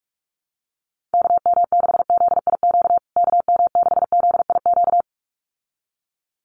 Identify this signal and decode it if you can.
Morse code, heard as a keyed tone, about 44 wpm, 700 Hz. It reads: KM6ZIQ KM6ZIQ
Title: USA Call Signs Repeated 2x 45wpm